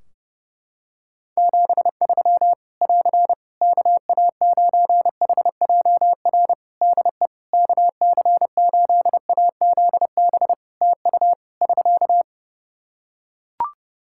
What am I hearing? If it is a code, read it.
Morse code, heard as a keyed tone, about 30 wpm, 700 Hz. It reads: 73 <AR> KA9HJR DE KC8AZB TU <SK>
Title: Practice QSO - Exchange 4 - Ending - 30wpm